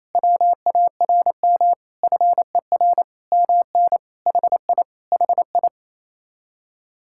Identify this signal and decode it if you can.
Morse code, heard as a keyed tone, about 29 wpm, 700 Hz. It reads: WARM FER MN HI HI
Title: Speed-Racing Common QSO Phrases - 28wpm